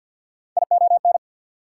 Morse code, heard as a keyed tone, about 49 wpm, 690 Hz. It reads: ION
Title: Prefixes and Suffixes 50wpm